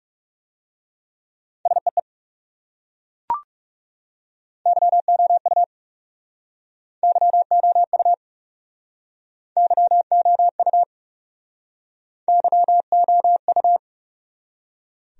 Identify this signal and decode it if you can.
Morse code, heard as a keyed tone, about 40 wpm, 700 Hz. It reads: SEE YOU YOU YOU YOU